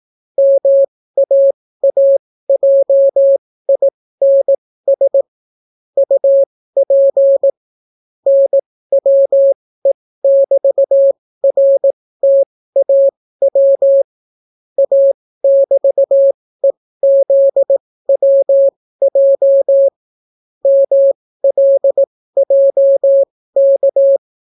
Morse code, heard as a keyed tone, about 18 wpm, spaced as for 14 wpm, 550 Hz. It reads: MAAJINS UP NWE<BT>RTAW A<BT>EZWJ MLJK